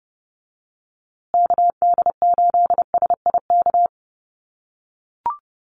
Morse code, heard as a keyed tone, about 30 wpm, 700 Hz. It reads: KD8SIK
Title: USA Call Signs Repeated 3x 30wpm